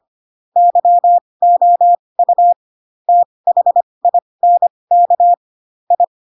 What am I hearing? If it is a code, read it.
Morse code, heard as a keyed tone, about 25 wpm, spaced as for 20 wpm, 700 Hz. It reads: YOU THINK I
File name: Pinocchio / Pinocchio ch2 15 25 ik1hge